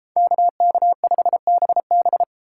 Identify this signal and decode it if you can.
Morse code, heard as a keyed tone, about 31 wpm, 700 Hz. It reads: KK5BB